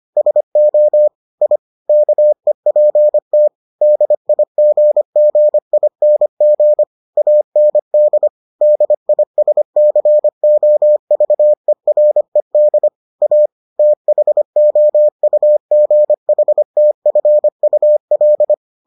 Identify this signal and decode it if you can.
Morse code, heard as a keyed tone, about 25 wpm, 600 Hz. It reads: SO I KEPT DIGGING AND DISCOVERED A THOUGHTFUL